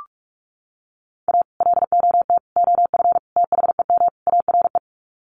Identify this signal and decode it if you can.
Morse code, heard as a keyed tone, about 44 wpm, 700 Hz. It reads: A LOT OF THEM ARE
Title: Sets of 5 Words Encore2 45wpm